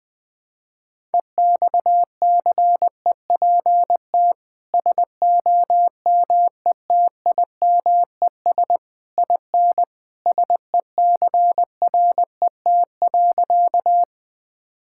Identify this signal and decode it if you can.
Morse code, heard as a keyed tone, about 21 wpm, 700 Hz. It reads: EXCEPT SOMETIMES IN SECRET.
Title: Sentences from Top 1600 Words 20wpm